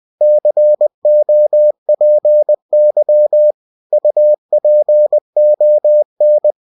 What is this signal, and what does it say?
Morse code, heard as a keyed tone, about 20 wpm, 600 Hz. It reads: COPY UPON
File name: cw-books / winnie-the-pooh / winnie-20wpm-page-0116